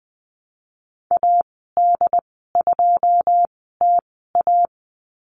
Morse code, heard as a keyed tone, about 20 wpm, spaced as for 15 wpm, 700 Hz. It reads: AD2TA
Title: 147 Lesson 24g - Letter y - Call Signs 20wpm